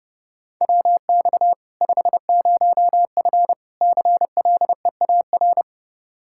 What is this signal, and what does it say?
Morse code, heard as a keyed tone, about 32 wpm, 700 Hz. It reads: WX 50F CLEAR
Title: K1USN SST QSO Practice 30wpm